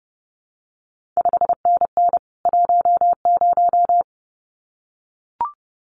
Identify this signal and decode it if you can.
Morse code, heard as a keyed tone, about 30 wpm, 700 Hz. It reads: 5NN 10
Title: CQ WW DX Contest Exchange Encore - 30wpm